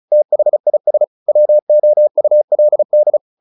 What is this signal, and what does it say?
Morse code, heard as a keyed tone, about 34 wpm, 600 Hz. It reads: THIS WOULD
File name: the old man and the sea 2 35wpm